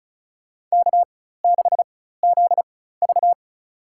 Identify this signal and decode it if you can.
Morse code, heard as a keyed tone, about 34 wpm, spaced as for 16 wpm, 700 Hz. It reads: K6ZV
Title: USA Call Signs - 4 Characters - 4x Character Spacing - 35wpm